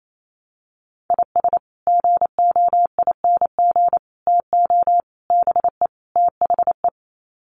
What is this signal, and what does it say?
Morse code, heard as a keyed tone, about 28 wpm, 700 Hz. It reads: IS GOING TO BE THE
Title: Sets of 5 Words 28wpm